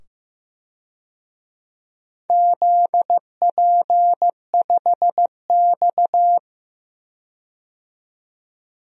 Morse code, heard as a keyed tone, about 15 wpm, 700 Hz. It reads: ZP5X
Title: DXCC-100 Call Signs - 4 Characters - 15wpm